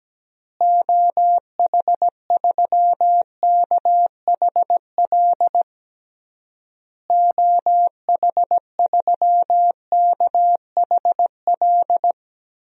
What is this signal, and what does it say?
Morse code, heard as a keyed tone, about 17 wpm, 700 Hz. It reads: OH3KHL OH3KHL
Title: DXCC-100 Call Signs - 6 Characters - Repeated 3x - 17wpm